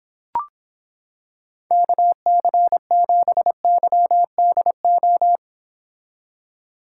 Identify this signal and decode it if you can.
Morse code, heard as a keyed tone, about 26 wpm, 700 Hz. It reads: KC7YDO